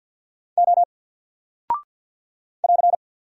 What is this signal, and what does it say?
Morse code, heard as a keyed tone, about 49 wpm, 700 Hz. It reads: O <AR>
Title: Single Character (Letter:Number:Punctuation:Prosign) 50wpm